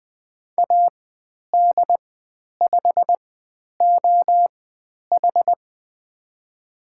Morse code, heard as a keyed tone, about 20 wpm, spaced as for 10 wpm, 700 Hz. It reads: AD5OH
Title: 102 Lesson 17g - Letter u - Call Signs 20wpm